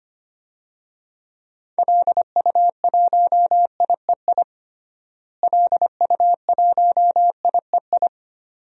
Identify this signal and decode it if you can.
Morse code, heard as a keyed tone, about 25 wpm, 700 Hz. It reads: LU1IEI LU1IEI
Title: DXCC-100 Call Signs - 6 Characters - Repeated 2x - 25wpm